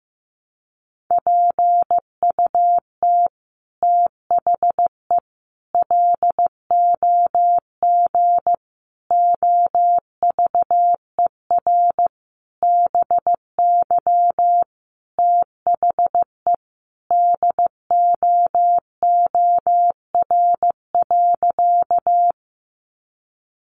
Morse code, heard as a keyed tone, about 15 wpm, 700 Hz. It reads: PUT THE LOG OVER BY THE DOOR.
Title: Sentences from Top 1400 Words 15wpm